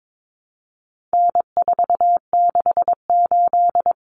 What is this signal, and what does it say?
Morse code, heard as a keyed tone, about 22 wpm, 700 Hz. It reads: N468
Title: Common Cut Numbers - 4-Digit 22wpm